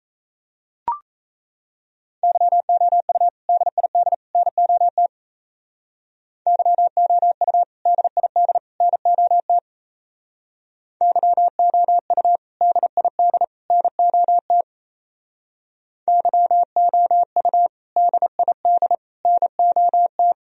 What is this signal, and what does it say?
Morse code, heard as a keyed tone, about 33 wpm, 700 Hz. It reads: YOU DID NOT YOU DID NOT YOU DID NOT YOU DID NOT
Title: Speed-Racing - Sets of 3 Words 28wpm